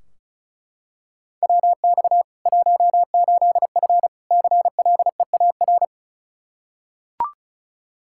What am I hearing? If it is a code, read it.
Morse code, heard as a keyed tone, about 36 wpm, 700 Hz. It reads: WX 18F CLEAR
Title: Practice QSO - Exchange 2 - Info - 35wpm